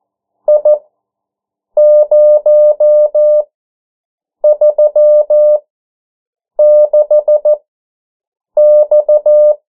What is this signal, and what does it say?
Morse code, heard as a keyed tone, about 14 wpm, spaced as for 7 wpm, 600 Hz. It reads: I036X
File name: morse mp3 / 09 urok 8